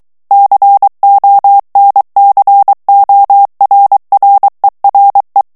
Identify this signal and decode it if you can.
Morse code, heard as a keyed tone, about 23 wpm, 800 Hz. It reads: CONCORRERE
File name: ITALIEN texte clair 2 vitesse 22 mots minute